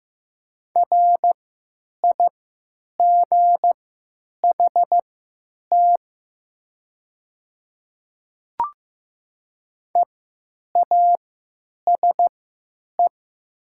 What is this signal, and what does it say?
Morse code, heard as a keyed tone, about 15 wpm, spaced as for 9 wpm, 700 Hz. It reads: RIGHT EASE
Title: 3x Character Spacing - Top 300 Words - 15wpm